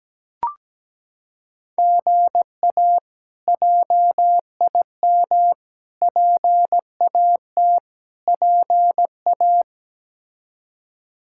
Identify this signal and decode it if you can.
Morse code, heard as a keyed tone, about 17 wpm, 700 Hz. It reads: GA JIM PAT PA
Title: K1USN SST QSO Practice - 17wpm